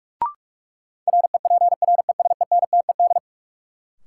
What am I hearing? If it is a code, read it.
Morse code, heard as a keyed tone, about 46 wpm, 700 Hz. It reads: REPRESENTED
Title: Sets of 5 Words - New Word Drill - 45wpm